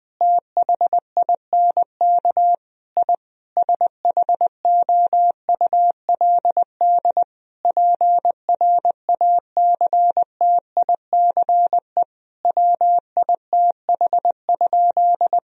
Morse code, heard as a keyed tone, about 20 wpm, 700 Hz. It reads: THINK I SHOULD PRACTICE WITH?
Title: Sentences from Top 1000 Words - Review Part 4 - 20wpm